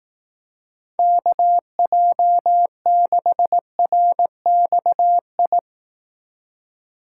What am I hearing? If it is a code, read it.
Morse code, heard as a keyed tone, about 18 wpm, 700 Hz. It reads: KJ6RXI
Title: Speed Racing - USA 2x3 Call Signs 15wpm